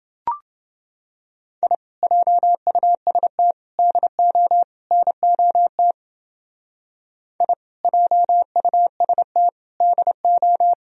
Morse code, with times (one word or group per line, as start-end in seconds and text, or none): 1.6-1.7 I
2.0-3.5 JUST
3.8-4.6 DO
4.9-5.9 NOT
7.4-7.5 I
7.8-9.5 JUST
9.8-10.7 DO